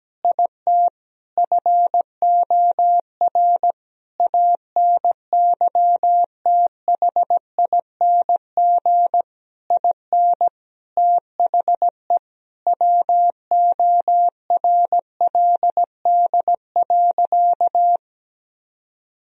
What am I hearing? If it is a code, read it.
Morse code, heard as a keyed tone, about 17 wpm, 700 Hz. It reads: IT FOR ANYTHING IN THE WORLD.
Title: Sentences from Top 1000 Words - Review Part 3 - 17wpm